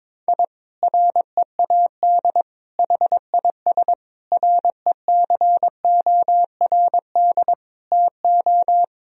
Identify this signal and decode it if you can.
Morse code, heard as a keyed tone, about 22 wpm, 700 Hz. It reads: I READ HIS RECORD TO
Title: Sentences from Top 1000 Words - Review Part 3 - 22wpm